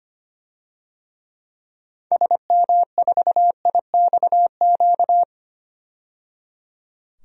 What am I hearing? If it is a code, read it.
Morse code, heard as a keyed tone, about 25 wpm, 700 Hz. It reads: SM4IXQ
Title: DXCC-100 Call Signs - 6 Characters - 25wpm